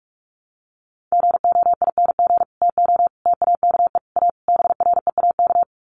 Morse code, heard as a keyed tone, about 44 wpm, 700 Hz. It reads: GOING TO TAKE A BREAK
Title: Sets of 5 Words 45wpm